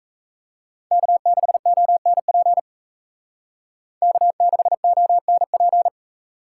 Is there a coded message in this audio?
Yes: Morse code from 0.9 to 5.9 s, heard as a keyed tone, about 36 wpm, 700 Hz. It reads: K6ONP K6ONP